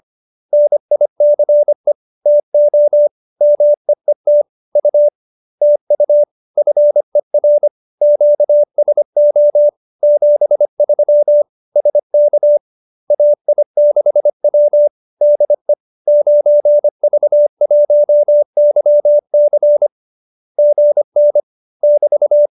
Morse code, heard as a keyed tone, about 25 wpm, 590 Hz. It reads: NICE TO MEET U TU FER QSO 73 SK AI6W DE 9V1YC GN <BT>
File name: qso207b 25